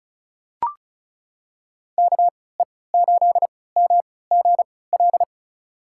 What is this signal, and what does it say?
Morse code, heard as a keyed tone, about 36 wpm, spaced as for 20 wpm, 700 Hz. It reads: KE8MGL
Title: USA Call Signs - 3x Character Spacing - 35wpm